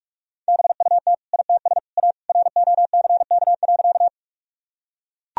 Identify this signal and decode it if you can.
Morse code, heard as a keyed tone, about 41 wpm, 700 Hz. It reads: BUT ITS A ROCK.
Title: Sentences from Top 1000 Words - Review Part 2 - 45wpm